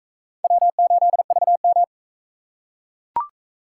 Morse code, heard as a keyed tone, about 38 wpm, 700 Hz. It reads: W8VM